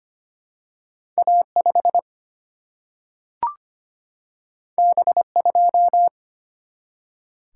Morse code, heard as a keyed tone, about 25 wpm, 700 Hz. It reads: A5 B2
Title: Single-Letter Single-Number 25wpm